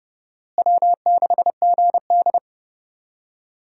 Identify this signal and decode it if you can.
Morse code, heard as a keyed tone, about 30 wpm, 700 Hz. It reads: W6GD
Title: USA Call Signs - 4 Characters - Repeated 2x - 30wpm